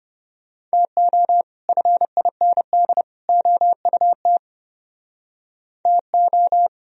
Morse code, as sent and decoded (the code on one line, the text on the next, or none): - --- / ..-. .. -. -.. / --- ..- - / - ---
TO FIND OUT TO